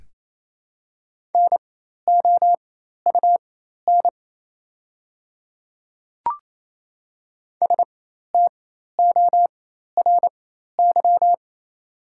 Morse code, heard as a keyed tone, about 28 wpm, spaced as for 13 wpm, 690 Hz. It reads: NOUN STORY
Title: 4x Character Spacing - Top 500 Words - 28wpm